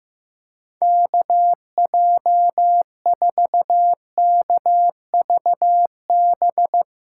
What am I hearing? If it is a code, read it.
Morse code, heard as a keyed tone, about 15 wpm, 700 Hz. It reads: KJ4KVB
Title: USA Call Signs - 6 Characters - 8x Character Spacing - 15wpm